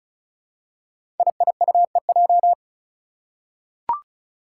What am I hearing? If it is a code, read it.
Morse code, heard as a keyed tone, about 34 wpm, 700 Hz. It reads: IIUEJ
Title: Five Challenging Letter-Number Combinations 35wpm